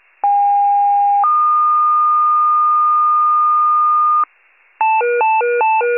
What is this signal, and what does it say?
Two-tone page, heard as a one-second tone, then a three-second tone, about 810 then 1220 Hz. It is heard from 0.2 to 4.2 s.